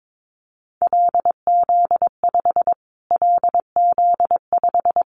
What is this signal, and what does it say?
Morse code, heard as a keyed tone, about 22 wpm, 700 Hz. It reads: LZ5 LZ5
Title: Callsign Prefixes for Top 25 DXCC - Repeated 3x - 22wpm